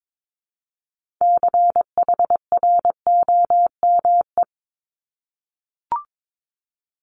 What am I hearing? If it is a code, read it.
Morse code, heard as a keyed tone, about 22 wpm, 700 Hz. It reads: CHROME
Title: Multi-letter Phonemes Repeated 3x 22wpm